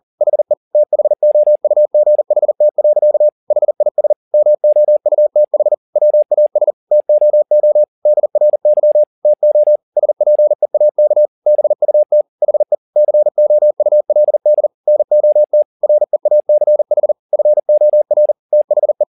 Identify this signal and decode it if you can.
Morse code, heard as a keyed tone, about 40 wpm, 600 Hz. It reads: HE THOUGHT. HIS MOUTH WAS TOO DRY TO SPEAK BUT HE COULD NOT REACH FOR THE
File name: the old man and the sea 2 40wpm